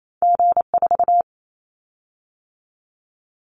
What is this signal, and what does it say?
Morse code, heard as a keyed tone, about 28 wpm, 700 Hz. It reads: G4
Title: Callsign Prefixes for Top 25 DXCC - 28wpm